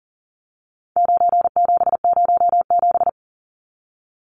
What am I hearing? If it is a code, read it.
Morse code, heard as a keyed tone, about 40 wpm, 700 Hz. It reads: 9707
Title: Numbers - 4-Digit 40wpm